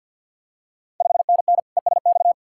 Morse code, heard as a keyed tone, about 51 wpm, 690 Hz. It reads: 5NN ESX
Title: Ontario QSO Party Exchange 50wpm